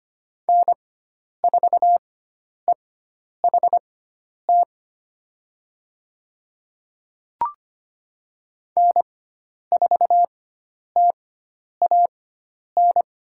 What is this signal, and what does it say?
Morse code, heard as a keyed tone, about 25 wpm, spaced as for 10 wpm, 700 Hz. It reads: N4EHT N4TAN
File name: Lesson 16g - Letter c - Call Signs 25wpm